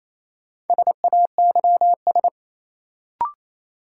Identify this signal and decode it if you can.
Morse code, heard as a keyed tone, about 28 wpm, 700 Hz. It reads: SAYS
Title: Sets of 2 Words - New Word Drill - 28wpm